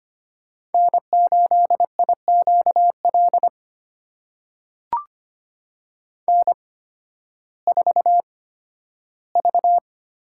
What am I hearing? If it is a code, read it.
Morse code, heard as a keyed tone, about 25 wpm, spaced as for 13 wpm, 700 Hz. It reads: N8IQL N4V
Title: USA Call Signs - 5 Characters - 8x Character Spacing - 25wpm